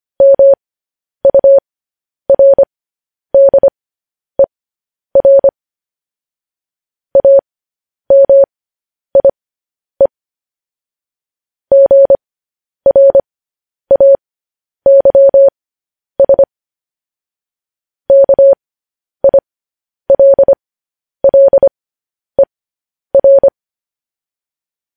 Morse code, heard as a keyed tone, about 25 wpm, spaced as for 10 wpm, 550 Hz. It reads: MURDER AMIE GRAYS KILLER
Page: news-headlines-in-morse-code-at-10-wpm